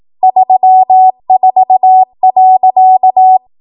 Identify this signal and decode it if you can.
Morse code, heard as a keyed tone, about 18 wpm, 750 Hz. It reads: 34.
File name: Archive / 18 WPM / 18WPM